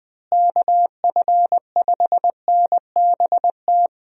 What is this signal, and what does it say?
Morse code, heard as a keyed tone, about 20 wpm, 700 Hz. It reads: KF5NBT
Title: USA Call Signs - 6 Characters - Repeated 3x - 20wpm